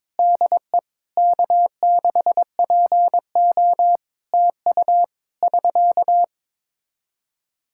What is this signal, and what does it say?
Morse code, heard as a keyed tone, about 22 wpm, 700 Hz. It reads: DE K6PO TU <SK>
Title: Practice QSO - Exchange 4 - Ending - 22wpm